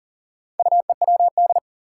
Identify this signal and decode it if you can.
Morse code, heard as a keyed tone, about 40 wpm, 700 Hz. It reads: UEWD
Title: Four Challenging Letter-Number Combinations 40wpm